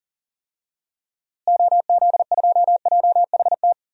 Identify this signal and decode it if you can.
Morse code, heard as a keyed tone, about 40 wpm, 700 Hz. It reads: OZ2JHT
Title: DXCC-100 Call Signs - 6 Characters - Repeated 2x - 40wpm